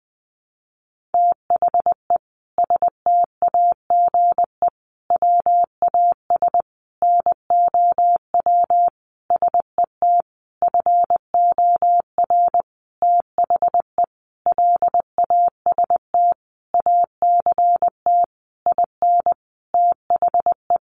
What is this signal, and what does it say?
Morse code, heard as a keyed tone, about 19 wpm, 700 Hz. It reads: THE STAGE WAS NOW SET FOR THE LAST ACT IN THE
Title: Sentences from Top 1500 Words 20wpm